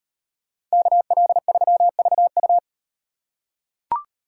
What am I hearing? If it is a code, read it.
Morse code, heard as a keyed tone, about 38 wpm, 700 Hz. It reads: KL3VU